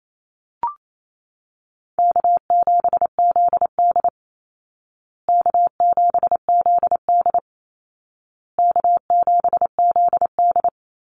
Morse code, heard as a keyed tone, about 28 wpm, 700 Hz. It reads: K7ZD K7ZD K7ZD
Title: Easy Call Signs Repeated 3x - 28wpm